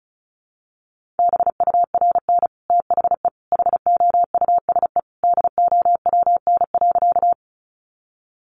Morse code, heard as a keyed tone, about 34 wpm, 700 Hz. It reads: BURN THE HOUSE DOWN.
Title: Sentences from Top 1300 Words 35wpm